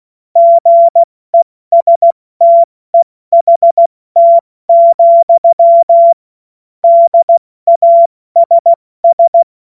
Morse code, heard as a keyed tone, about 16 wpm, spaced as for 14 wpm, 680 Hz. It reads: GESTEHT, DASS
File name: Morsekurs / Witze / Witz 28.November 2023 70 F80